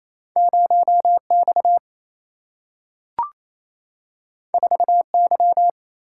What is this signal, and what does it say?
Morse code, heard as a keyed tone, about 28 wpm, 700 Hz. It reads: 0 X 4 Y
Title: Single-Letter Single-Number 28wpm